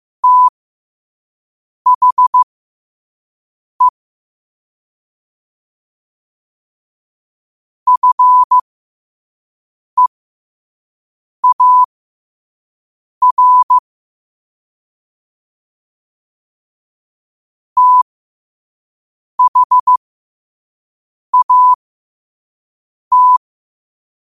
Morse code, heard as a keyed tone, about 15 wpm, spaced as for 5 wpm, 1000 Hz. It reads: THE FEAR THAT
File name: qotdfiles / 5wpm